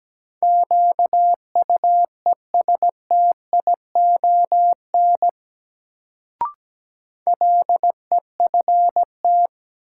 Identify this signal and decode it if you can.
Morse code, heard as a keyed tone, about 17 wpm, 700 Hz. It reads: QUESTION LEFT
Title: Sets of 5 Words Encore - New Word Drill - 17wpm